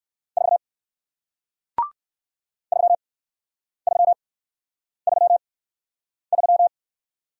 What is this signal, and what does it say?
Morse code, heard as a keyed tone, about 58 wpm, spaced as for 9 wpm, 700 Hz. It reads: 4 3333